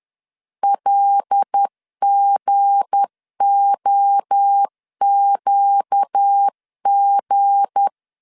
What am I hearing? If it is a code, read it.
Morse code, heard as a keyed tone, about 10 wpm, 800 Hz. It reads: LGOQG
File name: LEZIONI / 7 GF9Z / 7 G 10wpm 2501